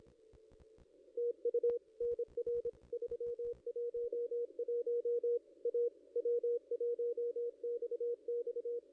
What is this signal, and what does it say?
Morse code, heard as a keyed tone, about 28 wpm, 440 Hz. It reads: TU NR 311 A W1XX
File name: contests / ss-cw-k5tr-op-n5rz -1700